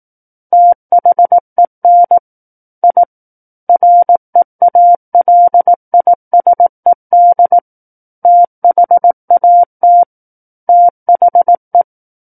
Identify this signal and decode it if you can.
Morse code, heard as a keyed tone, about 18 wpm, 700 Hz. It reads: THEN I REALISED THAT THE
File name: 18 WPM